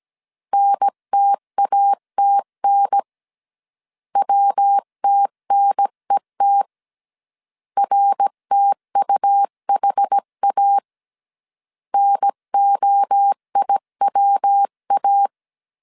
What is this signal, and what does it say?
Morse code, heard as a keyed tone, about 17 wpm, 800 Hz. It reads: NTATN WTNET RTUHA NOIWA
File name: CW / DET / 1 3 1500C S A 16wpm 2141